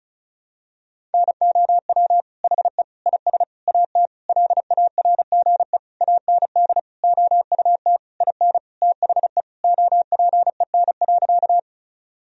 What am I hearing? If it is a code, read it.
Morse code, heard as a keyed tone, about 36 wpm, 700 Hz. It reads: NOW HE IS AT LARGE AND OUT IN THE OPEN.
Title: Sentences from Top 1000 Words - Review Part 1 - 35wpm